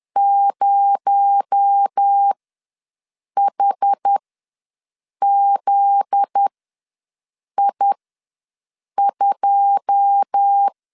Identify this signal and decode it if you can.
Morse code, heard as a keyed tone, about 10 wpm, spaced as for 6 wpm, 800 Hz. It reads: 0HZI2
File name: LEZIONI / 9 2 ws 10wpm 4130